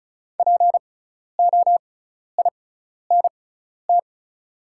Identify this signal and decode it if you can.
Morse code, heard as a keyed tone, about 36 wpm, spaced as for 12 wpm, 700 Hz. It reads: POINT
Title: 6x Character Spacing - Top 300 Words - 35wpm